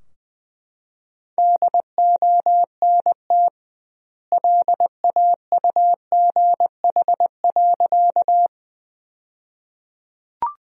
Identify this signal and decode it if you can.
Morse code, heard as a keyed tone, about 20 wpm, 700 Hz. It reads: DONT LAUGH.
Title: 2x Word-Spacing - Sentences from Top 500 Words - 20wpm